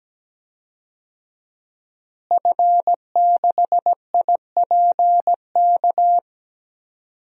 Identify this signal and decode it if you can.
Morse code, heard as a keyed tone, about 17 wpm, 700 Hz. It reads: F6IPK
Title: DXCC-100 Call Signs - 5 Characters - Repeated 2x - 17wpm